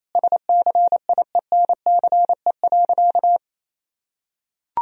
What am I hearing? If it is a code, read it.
Morse code, heard as a keyed tone, about 28 wpm, 700 Hz. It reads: SCIENCE.
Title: Sentences from Top 1000 Words - Review Part 2 - 28wpm